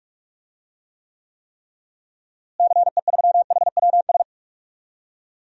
Morse code, heard as a keyed tone, about 46 wpm, 700 Hz. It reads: CE3HWS
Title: DXCC-100 Call Signs - 6 Characters - Repeated 3x - 45wpm